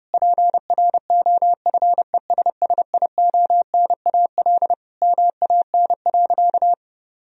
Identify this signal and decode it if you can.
Morse code, heard as a keyed tone, about 30 wpm, 700 Hz. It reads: PROFESSIONAL MAN.
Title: Sentences from Top 1000 Words - Review Part 4 - 30wpm